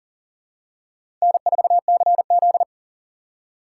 Morse code, heard as a keyed tone, about 38 wpm, 700 Hz. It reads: N4CZ
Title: USA Call Signs - 4 Characters - 4x Character Spacing - 40wpm